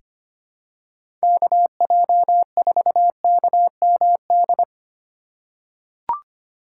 Morse code, heard as a keyed tone, about 25 wpm, 700 Hz. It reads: KJ4KMD
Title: USA Call Signs - 6 Characters - 4x Character Spacing - 25wpm